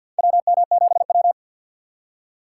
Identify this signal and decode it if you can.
Morse code, heard as a keyed tone, about 48 wpm, 700 Hz. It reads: WM7W
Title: USA Call Signs - 4 Characters - 2x Character Spacing - 50wpm